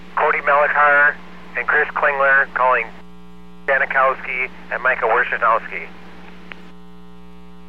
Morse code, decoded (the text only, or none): none